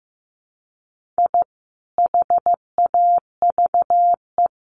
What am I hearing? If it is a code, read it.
Morse code, heard as a keyed tone, about 15 wpm, 700 Hz. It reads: I HAVE
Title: Sets of 3 Words 15wpm